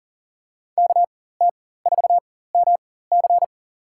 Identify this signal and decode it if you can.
Morse code, heard as a keyed tone, about 40 wpm, spaced as for 19 wpm, 700 Hz. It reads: KT4MC
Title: USA Call Signs - 5 Characters - 4x Character Spacing - 40wpm